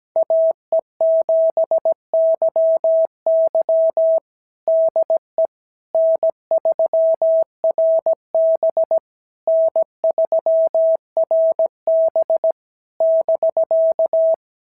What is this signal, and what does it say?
Morse code, heard as a keyed tone, about 16 wpm, 650 Hz. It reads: AE7YY DE N3RB N3RB <BK>
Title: SOTA QSO Practice 17wpm